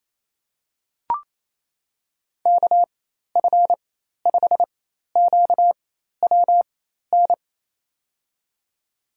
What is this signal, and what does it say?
Morse code, heard as a keyed tone, about 28 wpm, spaced as for 13 wpm, 700 Hz. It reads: KF5QWN
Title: USA Call Signs - 6 Characters - 4x Character Spacing - 28wpm